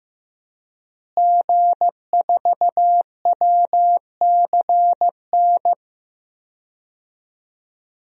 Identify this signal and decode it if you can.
Morse code, heard as a keyed tone, about 15 wpm, 700 Hz. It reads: G4WCN
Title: DXCC-100 Call Signs - 5 Characters - Repeated 2x - 15wpm